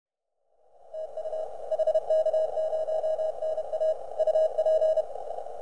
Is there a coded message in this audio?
Yes: Morse code from 0.9 to 5.0 s, heard as a keyed tone, about 32 wpm, 630 Hz. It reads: TU HK0NA UP